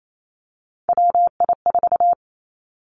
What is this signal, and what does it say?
Morse code, heard as a keyed tone, about 28 wpm, 700 Hz. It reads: WI4
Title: Three Challenging Letter-Number Combinations 28wpm